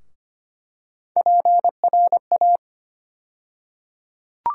Morse code, heard as a keyed tone, about 25 wpm, 700 Hz. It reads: PRA
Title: USA Call Sign Postfixes - 3 Letters 25wpm